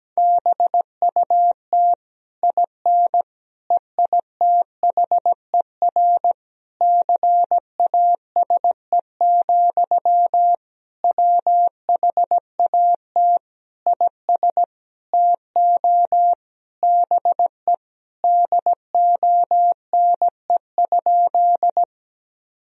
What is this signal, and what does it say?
Morse code, heard as a keyed tone, about 17 wpm, 700 Hz. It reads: BUT IN EITHER CASE, WHAT IS TO BE DONE?
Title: Sentences from Top 1000 Words - Review Part 4 - 17wpm